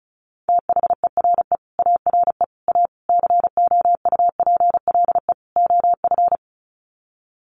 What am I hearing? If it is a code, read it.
Morse code, heard as a keyed tone, about 34 wpm, 700 Hz. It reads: THERE ARE A COUPLE OF
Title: Sets of 5 Words Encore 35wpm